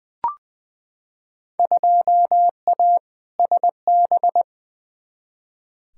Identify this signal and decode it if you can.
Morse code, heard as a keyed tone, about 21 wpm, 700 Hz. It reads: 2A SB
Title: Field Day Exchanges - 20wpm